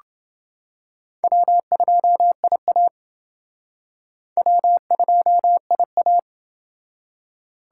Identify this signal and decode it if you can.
Morse code, heard as a keyed tone, about 28 wpm, 700 Hz. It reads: W2IA W2IA